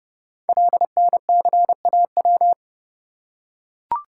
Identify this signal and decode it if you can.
Morse code, heard as a keyed tone, about 29 wpm, 700 Hz. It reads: LNCAW
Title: Code Groups - 5 Letters - 30wpm